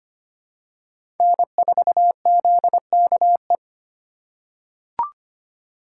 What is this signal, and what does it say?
Morse code, heard as a keyed tone, about 25 wpm, 700 Hz. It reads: N4ZKE
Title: USA Call Signs - 5 Characters - 4x Character Spacing - 25wpm